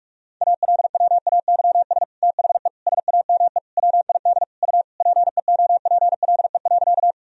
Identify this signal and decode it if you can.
Morse code, heard as a keyed tone, about 41 wpm, 700 Hz. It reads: ALWAYS THE SAME WID U PEOPLE.
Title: Sentences from Top 300 Words and Abbreviations 45wpm